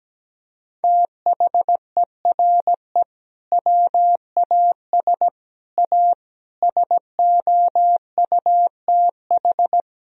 Morse code, heard as a keyed tone, about 17 wpm, 700 Hz. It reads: THERE WAS A SOUTH
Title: Sentences from Top 1000 Words - Review Part 2 - 17wpm